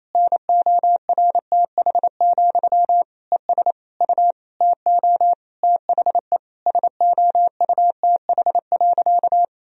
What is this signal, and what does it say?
Morse code, heard as a keyed tone, about 28 wpm, 700 Hz. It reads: NORTH, ES U TO THE SOUTH.
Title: Sentences from Top 400 Words and Abbreviations 28wpm